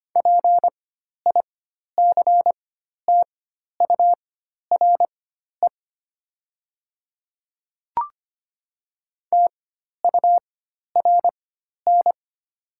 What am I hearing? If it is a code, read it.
Morse code, heard as a keyed tone, about 25 wpm, spaced as for 12 wpm, 700 Hz. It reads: PICTURE TURN
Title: 4x Character Spacing - Top 200 Words - 25wpm